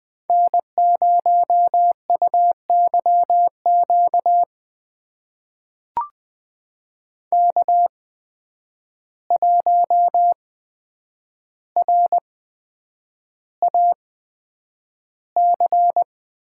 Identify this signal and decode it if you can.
Morse code, heard as a keyed tone, about 20 wpm, spaced as for 9 wpm, 700 Hz. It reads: N0UYQ K1RAC
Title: USA Call Signs - 5 Characters - 8x Character Spacing - 20wpm